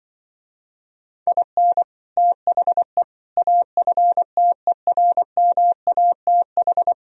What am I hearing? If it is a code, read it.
Morse code, heard as a keyed tone, about 24 wpm, 700 Hz. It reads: IN THE AFTERMATH
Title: Speed-Racing - Sets of 5 Words 20wpm